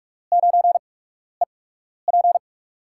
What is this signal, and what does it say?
Morse code, heard as a keyed tone, about 44 wpm, spaced as for 12 wpm, 700 Hz. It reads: 9EP
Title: USA Call Signs - 5 Characters - 8x Character Spacing - 45wpm